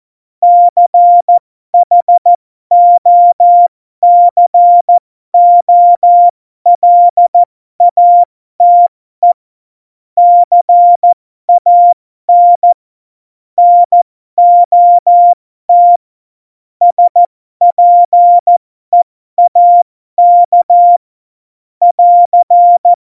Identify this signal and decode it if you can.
Morse code, heard as a keyed tone, about 14 wpm, spaced as for 12 wpm, 700 Hz. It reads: CHOCOLATE CAN NOT SPEAK <AR>
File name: Morsekurs / Witze